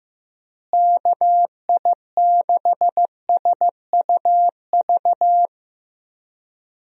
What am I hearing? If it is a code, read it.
Morse code, heard as a keyed tone, about 15 wpm, 700 Hz. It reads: KI6SUV
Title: USA Call Signs - 6 Characters - Repeated 2x - 15wpm